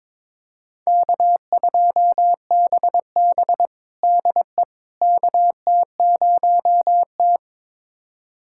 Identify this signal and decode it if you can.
Morse code, heard as a keyed tone, about 22 wpm, 700 Hz. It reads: K2BB DE KT0T
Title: USA Callsign DE Callsign - 4 Characters - 22wpm